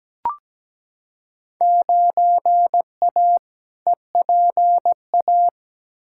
Morse code, heard as a keyed tone, about 17 wpm, 700 Hz. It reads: 9A EPA
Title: Field Day Exchanges Encore - 17wpm